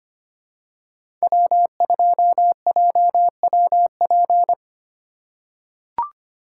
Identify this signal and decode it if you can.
Morse code, heard as a keyed tone, about 25 wpm, 700 Hz. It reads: W2JWP
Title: USA Call Signs - 5 Characters - 3x Character Spacing - 25wpm